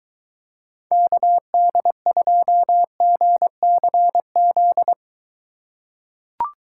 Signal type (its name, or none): Morse code